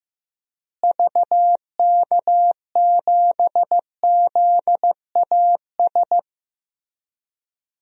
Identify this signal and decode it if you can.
Morse code, heard as a keyed tone, about 15 wpm, 700 Hz. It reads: VK7ZAS
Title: DXCC-100 Call Signs - 6 Characters - Repeated 2x - 15wpm